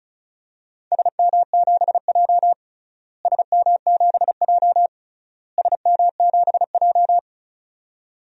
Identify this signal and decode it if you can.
Morse code, heard as a keyed tone, about 34 wpm, 700 Hz. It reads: SM7J SM7J SM7J
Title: DXCC-100 Call Signs - 4 Characters - Repeated 3x - 35wpm